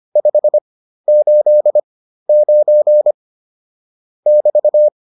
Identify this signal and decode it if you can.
Morse code, heard as a keyed tone, about 25 wpm, spaced as for 13 wpm, 600 Hz. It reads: 589 <BT>